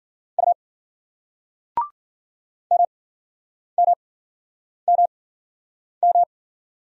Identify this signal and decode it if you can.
Morse code, heard as a keyed tone, about 50 wpm, spaced as for 9 wpm, 700 Hz. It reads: U MMMM
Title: Speed-Racing Single Letters 40wpm